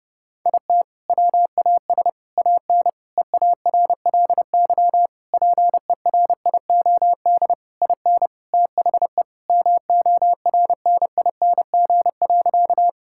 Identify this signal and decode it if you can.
Morse code, heard as a keyed tone, about 30 wpm, 700 Hz. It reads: IT WAS AN EARLY PERIOD IN THE MORNING.
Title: Sentences from Top 1000 Words - Review Part 5 - 30wpm